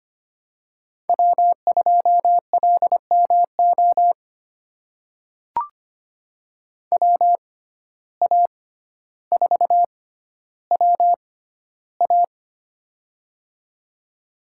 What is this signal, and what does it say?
Morse code, heard as a keyed tone, about 25 wpm, spaced as for 13 wpm, 700 Hz. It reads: W2LMO WA4WA
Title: USA Call Signs - 5 Characters - 6x Character Spacing - 25wpm